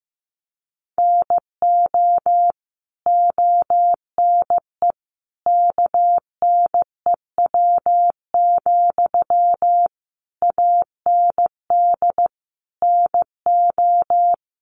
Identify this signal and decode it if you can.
Morse code, heard as a keyed tone, about 15 wpm, 700 Hz. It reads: NO ONE KNEW, AND NO
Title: Sentences from Top 1400 Words 15wpm